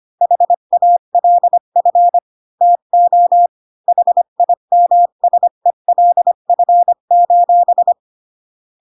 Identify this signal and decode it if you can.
Morse code, heard as a keyed tone, about 25 wpm, 700 Hz. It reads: HALF TO HIMSELF:
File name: Pinocchio / Pinocchio ch1 20 25 ik1hge